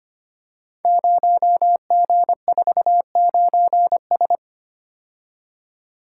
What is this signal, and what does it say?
Morse code, heard as a keyed tone, about 25 wpm, 700 Hz. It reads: 0G49S
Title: Code Groups - 5 Letter:Numbers - 25wpm